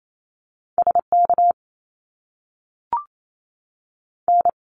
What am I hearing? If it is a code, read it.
Morse code, heard as a keyed tone, about 28 wpm, spaced as for 6 wpm, 700 Hz. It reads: SKN
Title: Canadian Provinces and Territories 28wpm